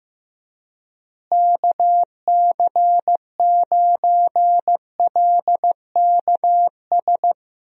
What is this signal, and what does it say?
Morse code, heard as a keyed tone, about 15 wpm, 700 Hz. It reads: KC9LKS
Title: USA Call Signs - 6 Characters - Repeated 3x - 15wpm